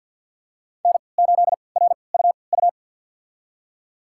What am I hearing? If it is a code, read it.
Morse code, heard as a keyed tone, about 47 wpm, spaced as for 28 wpm, 700 Hz. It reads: N8RUU
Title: USA Call Signs - 5 Characters - 3x Character Spacing - 50wpm